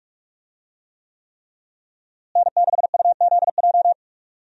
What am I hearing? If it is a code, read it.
Morse code, heard as a keyed tone, about 44 wpm, 700 Hz. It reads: N6UZJ
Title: USA Call Signs - 5 Characters - Repeated 3x - 45wpm